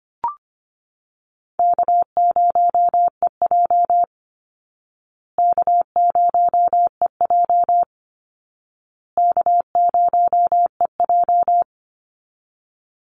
Morse code, heard as a keyed tone, about 25 wpm, 700 Hz. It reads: K0EJ K0EJ K0EJ
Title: Call Signs Repeated 2x - 25wpm